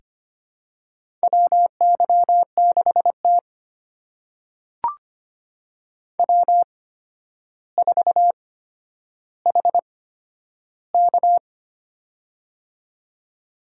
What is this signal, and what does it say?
Morse code, heard as a keyed tone, about 25 wpm, spaced as for 11 wpm, 700 Hz. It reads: WY6T W4HK
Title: USA Call Signs - 4 Characters - 8x Character Spacing - 25wpm